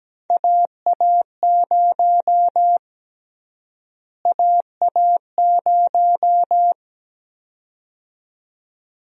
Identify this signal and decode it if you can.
Morse code, heard as a keyed tone, about 17 wpm, 700 Hz. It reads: AA0 AA0
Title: USA Call Sign Prefixes - Repeated 2x - 17wpm